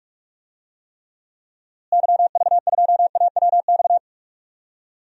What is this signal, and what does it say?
Morse code, heard as a keyed tone, about 44 wpm, 700 Hz. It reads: YV2AWX
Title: DXCC-100 Call Signs - 6 Characters - Repeated 2x - 45wpm